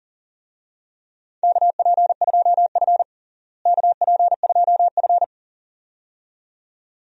Morse code, heard as a keyed tone, about 40 wpm, 700 Hz. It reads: KP2F KP2F
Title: DXCC-100 Call Signs - 4 Characters - Repeated 2x - 40wpm